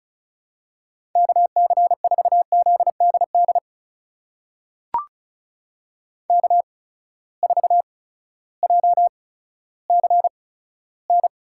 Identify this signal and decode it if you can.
Morse code, heard as a keyed tone, about 36 wpm, spaced as for 16 wpm, 700 Hz. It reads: KC4ZDD K4JCN
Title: USA Call Signs - 8x Character Spacing - 35wpm